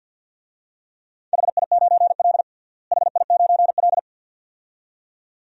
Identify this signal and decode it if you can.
Morse code, heard as a keyed tone, about 52 wpm, 700 Hz. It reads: HI9L HI9L
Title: DXCC-100 Call Signs - 4 Characters - Repeated 2x - 50wpm